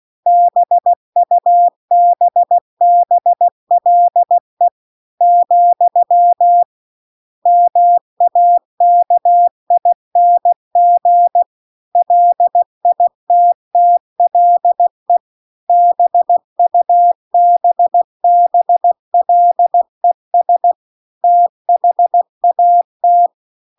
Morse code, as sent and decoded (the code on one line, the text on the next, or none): -... ..- -... -... .-.. / --..-- / -- .- -.- .. -. --. / .-.. .. - - .-.. / -... ..- -... -... .-.. ... / - .... .- -
BUBBLE , MAKING LITTLE BUBBLES THAT